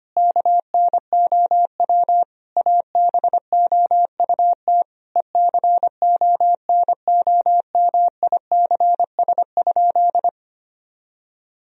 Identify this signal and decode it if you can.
Morse code, heard as a keyed tone, about 25 wpm, 700 Hz. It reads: KNOW ABOUT ECONOMICS?
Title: Sentences from Top 1000 Words - Review Part 3 - 25wpm